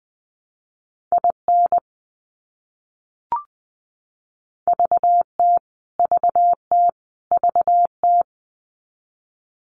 Morse code, heard as a keyed tone, about 20 wpm, 700 Hz. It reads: IN VT VT VT
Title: US States Repeated 3x 20wpm